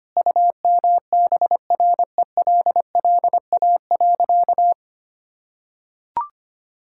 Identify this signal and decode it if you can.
Morse code, heard as a keyed tone, about 25 wpm, 700 Hz. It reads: UMBRELLA.
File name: practice / 25wpm / Sentences with 3-Word Phrases - 25wpm